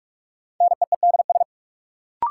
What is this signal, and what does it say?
Morse code, heard as a keyed tone, about 45 wpm, 700 Hz. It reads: NEEDS
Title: Sets of 3 Words - New Word Drill - 45wpm